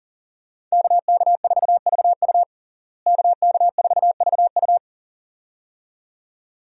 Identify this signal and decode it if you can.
Morse code, heard as a keyed tone, about 40 wpm, 700 Hz. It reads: KK4VU KK4VU
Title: USA Call Signs - 5 Characters - Repeated 2x - 40wpm